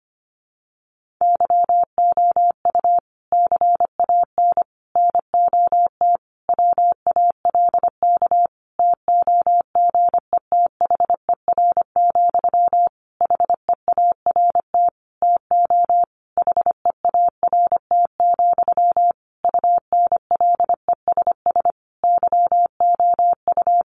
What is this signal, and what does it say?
Morse code, heard as a keyed tone, about 25 wpm, 700 Hz. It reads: YOU CAN NOT WALK TOGETHER, HEART TO HEART, UNLESS YOU
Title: Sentences from Top 1500 Words 25wpm